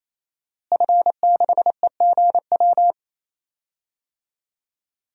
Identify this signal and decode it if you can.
Morse code, heard as a keyed tone, about 28 wpm, 700 Hz. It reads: F6EGW
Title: DXCC-100 Call Signs - 5 Characters - Repeated 3x - 28wpm